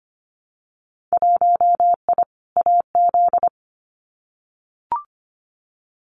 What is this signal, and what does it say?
Morse code, heard as a keyed tone, about 25 wpm, 690 Hz. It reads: 1I AZ
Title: Winter Field Day Exchanges 25wpm